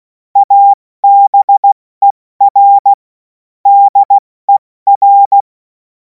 Morse code, heard as a keyed tone, about 16 wpm, 800 Hz. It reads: ABER DER
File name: Morsekurs / Witze / Witz 24. März 2026 70 F80